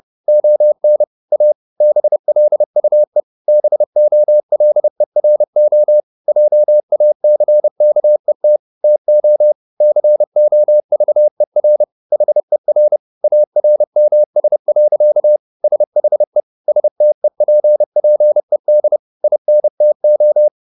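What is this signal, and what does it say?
Morse code, heard as a keyed tone, about 30 wpm, 600 Hz. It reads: ON A BLUE BOLERO JACKET TO COVER HER ARMS. SHE STEPPED INTO
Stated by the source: Morse code at 30 wpm, the novel Never by Ken Follett in English, from CW med Gnister.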